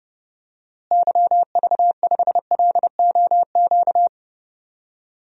Morse code, heard as a keyed tone, about 30 wpm, 700 Hz. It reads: YV5LOQ
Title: DXCC-100 Call Signs - 6 Characters - Repeated 2x - 30wpm